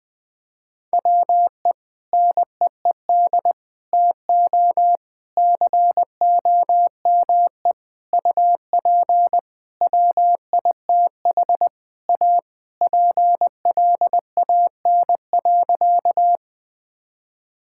Morse code, heard as a keyed tone, about 21 wpm, 700 Hz. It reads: WE NEED TO COME UP WITH A PLAN.
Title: Sentences with 5-Word Phrases - Spoken Beginning - 20wpm